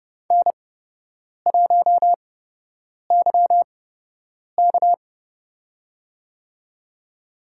Morse code, heard as a keyed tone, about 30 wpm, spaced as for 8 wpm, 700 Hz. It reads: N1YK
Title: USA Call Signs - 4 Characters - 8x Character Spacing - 30wpm